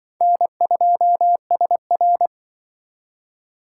Morse code, heard as a keyed tone, about 25 wpm, 700 Hz. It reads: N2SR
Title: Speed Racing - USA 1x2 Call Signs 20wpm